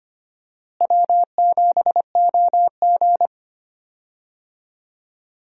Morse code, heard as a keyed tone, about 25 wpm, 700 Hz. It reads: W7OG
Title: USA Call Signs - 4 Characters - Repeated 3x - 25wpm